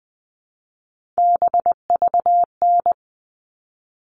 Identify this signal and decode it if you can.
Morse code, heard as a keyed tone, about 20 wpm, 700 Hz. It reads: BVN
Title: All Cut Numbers - 3-Digit 20wpm